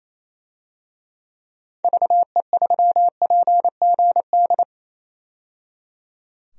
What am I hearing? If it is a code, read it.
Morse code, heard as a keyed tone, about 28 wpm, 700 Hz. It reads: VE3PGD
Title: DXCC-100 Call Signs - 6 Characters - 28wpm